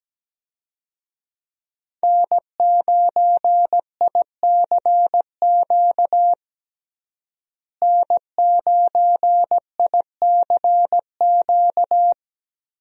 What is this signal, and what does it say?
Morse code, heard as a keyed tone, about 17 wpm, 700 Hz. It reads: N9ICQ N9ICQ
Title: USA Call Signs - 5 Characters - Repeated 2x - 17wpm